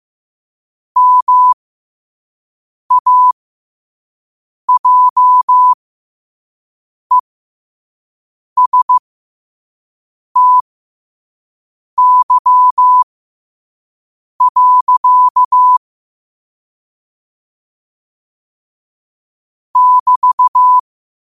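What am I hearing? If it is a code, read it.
Morse code, heard as a keyed tone, about 15 wpm, spaced as for 5 wpm, 1000 Hz. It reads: MAJESTY. <BT>
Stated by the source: Morse code at 5 wpm.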